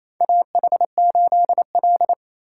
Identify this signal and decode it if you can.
Morse code, heard as a keyed tone, about 28 wpm, 700 Hz. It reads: AH8L
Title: USA Call Signs - 4 Characters - 4x Character Spacing - 28wpm